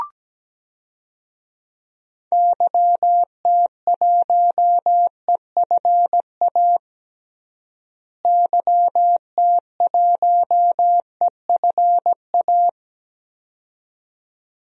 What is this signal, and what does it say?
Morse code, heard as a keyed tone, about 17 wpm, 700 Hz. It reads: YT1EFA YT1EFA
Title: DXCC-100 Call Signs - 6 Characters - Repeated 2x - 17wpm